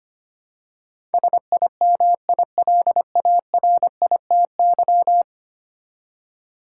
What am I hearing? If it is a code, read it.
Morse code, heard as a keyed tone, about 25 wpm, 700 Hz. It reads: SIMILARITY
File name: 10 Letter Words - First 3 Letters Given - 25wpm